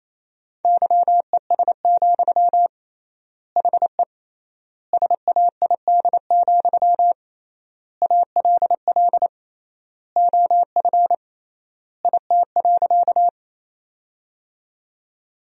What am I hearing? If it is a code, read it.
Morse code, heard as a keyed tone, about 28 wpm, 700 Hz. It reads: YES, HE SAID, ALL OF IT.
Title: 3x Word-Spacing - Sentences from Top 500 Words - 28wpm